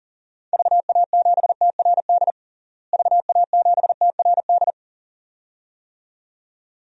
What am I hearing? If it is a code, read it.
Morse code, heard as a keyed tone, about 40 wpm, 700 Hz. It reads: VA7TRD VA7TRD
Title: DXCC-100 Call Signs - 6 Characters - Repeated 2x - 40wpm